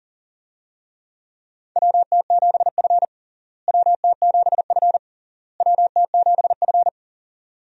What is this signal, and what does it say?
Morse code, heard as a keyed tone, about 40 wpm, 700 Hz. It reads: WT7F WT7F WT7F
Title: USA Call Signs - 4 Characters - Repeated 3x - 40wpm